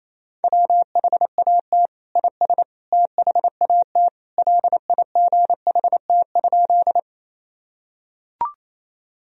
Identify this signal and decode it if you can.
Morse code, heard as a keyed tone, about 28 wpm, 700 Hz. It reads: WHAT IS THAT LIGHT?
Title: Sentences from Top 1000 Words - Review Part 1 - 28wpm